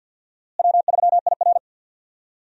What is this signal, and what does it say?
Morse code, heard as a keyed tone, about 51 wpm, 700 Hz. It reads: W3IR
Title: USA Call Signs - 4 Characters - Repeated 3x - 50wpm